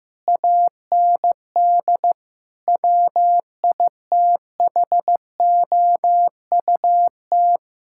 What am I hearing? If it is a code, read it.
Morse code, heard as a keyed tone, about 15 wpm, 700 Hz. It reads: AND WITHOUT
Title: Sentences from Top 1000 Words - Review Part 6 - 15wpm